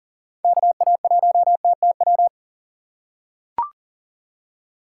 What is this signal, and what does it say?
Morse code, heard as a keyed tone, about 40 wpm, 700 Hz. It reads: KA1TTW
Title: USA Call Signs - 6 Characters - 4x Character Spacing - 40wpm